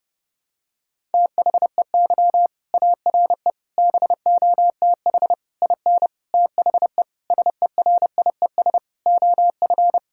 Morse code, heard as a keyed tone, about 32 wpm, 700 Hz. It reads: THEY ARE BOTH IN THE SERIES OF
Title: Sentences from Top 1000 Words - Review Part 5 - 30wpm